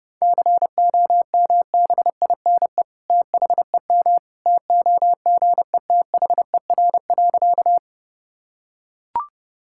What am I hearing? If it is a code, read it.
Morse code, heard as a keyed tone, about 32 wpm, 700 Hz. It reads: COMBINE THEM TOGETHER.
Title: Sentences from Top 1600 Words 30wpm